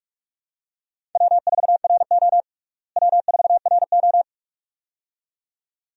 Morse code, heard as a keyed tone, about 46 wpm, 700 Hz. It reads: W4RO W4RO
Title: USA Call Signs - 4 Characters - Repeated 2x - 45wpm